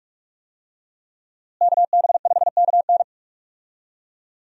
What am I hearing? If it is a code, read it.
Morse code, heard as a keyed tone, about 46 wpm, 700 Hz. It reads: KB5KN
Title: USA Call Signs - 5 Characters - Repeated 2x - 45wpm